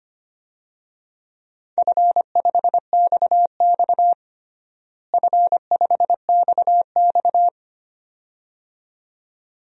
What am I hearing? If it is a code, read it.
Morse code, heard as a keyed tone, about 25 wpm, 700 Hz. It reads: F5XX F5XX
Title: DXCC-100 Call Signs - 4 Characters - Repeated 2x - 25wpm